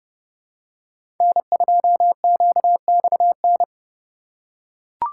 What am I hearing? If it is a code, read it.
Morse code, heard as a keyed tone, about 31 wpm, 700 Hz. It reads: N2QXN
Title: USA Call Signs - 5 Characters - 2x Character Spacing - 30wpm